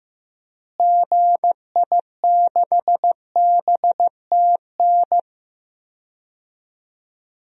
Morse code, heard as a keyed tone, about 15 wpm, 700 Hz. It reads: GI6BTN
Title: DXCC-100 Call Signs - 6 Characters - Repeated 3x - 15wpm